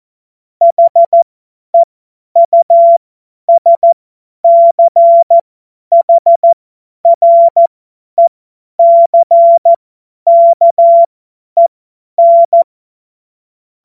Morse code, heard as a keyed tone, about 14 wpm, spaced as for 10 wpm, 680 Hz. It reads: HEUSCHRECKEN
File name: Morsekurs / Witze / Witz 28.November 2023 50 F70